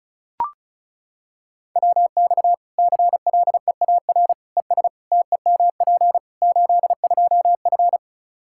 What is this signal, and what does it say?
Morse code, heard as a keyed tone, about 36 wpm, 700 Hz. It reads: WX CLEAR ES TEMP 82F
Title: Practice QSO - Exchange 2 - Info - 35wpm